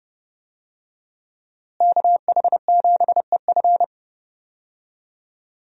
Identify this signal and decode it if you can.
Morse code, heard as a keyed tone, about 32 wpm, 700 Hz. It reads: KH7EF
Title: USA Call Signs - 5 Characters - Repeated 2x - 30wpm